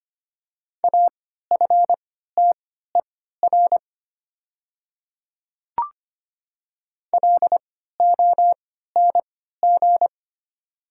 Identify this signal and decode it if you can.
Morse code, heard as a keyed tone, about 25 wpm, spaced as for 14 wpm, 690 Hz. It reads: AFTER LONG
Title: 3x Character Spacing - Top 200 Words - 25wpm